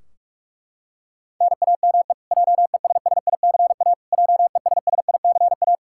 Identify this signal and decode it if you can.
Morse code, heard as a keyed tone, about 41 wpm, 700 Hz. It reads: NAME JESSICA JESSICA
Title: Practice QSO - Exchange 1 - Beginning - 45wpm